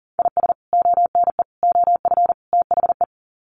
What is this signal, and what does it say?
Morse code, heard as a keyed tone, about 40 wpm, 700 Hz. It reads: IS ONE OF THE
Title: Sets of 4 Words 40wpm